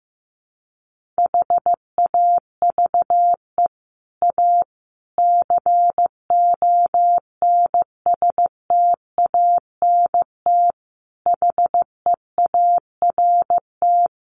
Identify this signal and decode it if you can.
Morse code, heard as a keyed tone, about 15 wpm, 700 Hz. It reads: HAVE A CONSTANT HEART
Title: Sentences from Top 1500 Words 15wpm